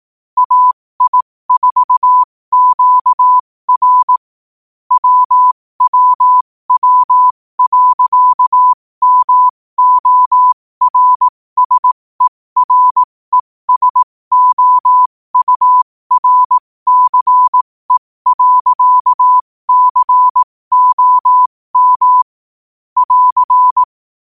Morse code, heard as a keyed tone, about 18 wpm, spaced as for 15 wpm, 1000 Hz. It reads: AI4QR WWW.MORSERESOURCE.COM <AR>
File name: qotdfiles / 15wpm